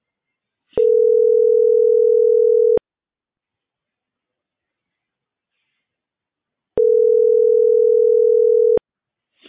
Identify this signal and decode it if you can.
Ringback tone (North America), 440 + 480 Hz.